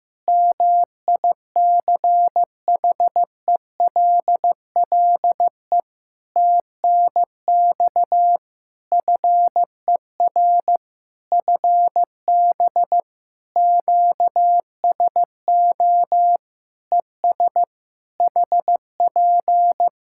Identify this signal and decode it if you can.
Morse code, heard as a keyed tone, about 15 wpm, 700 Hz. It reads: MICHELLE TNX FER FB QSO ES HP
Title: Practice QSO - Exchange 4 - Ending - 15wpm